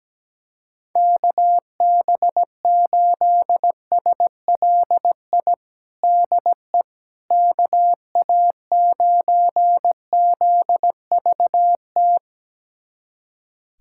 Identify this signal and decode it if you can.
Morse code, heard as a keyed tone, about 17 wpm, 700 Hz. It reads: KB8SLI DE KA9ZVT
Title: USA Callsign DE Callsign - 6 Characters - 17wpm